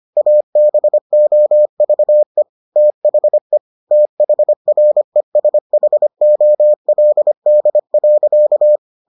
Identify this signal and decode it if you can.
Morse code, heard as a keyed tone, about 25 wpm, 600 Hz. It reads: ABOVE THE THRESHOLD.